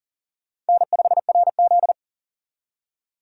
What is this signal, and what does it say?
Morse code, heard as a keyed tone, about 52 wpm, 700 Hz. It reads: N5RZ